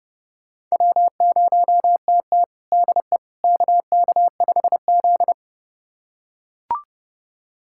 Morse code, heard as a keyed tone, about 30 wpm, 700 Hz. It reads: W0TT DE KK5Z
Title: USA Callsign DE Callsign - 4 Characters - 30wpm